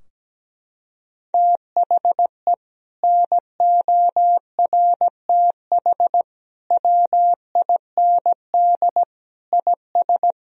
Morse code, heard as a keyed tone, about 17 wpm, 700 Hz. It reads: THE NORTH WIND IS
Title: Sentences from Top 1000 Words - Review Part 2 - 17wpm